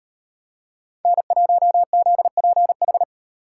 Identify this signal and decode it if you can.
Morse code, heard as a keyed tone, about 38 wpm, 700 Hz. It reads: N1ZPH